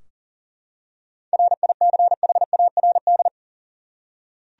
Morse code, heard as a keyed tone, about 40 wpm, 700 Hz. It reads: RICHARD
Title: USA Common Names - 40wpm